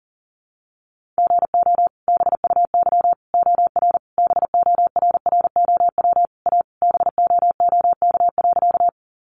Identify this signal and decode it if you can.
Morse code, heard as a keyed tone, about 40 wpm, 700 Hz. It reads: GO BUY OR BORROW A BOOK.
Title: Sentences from Top 1500 Words 40wpm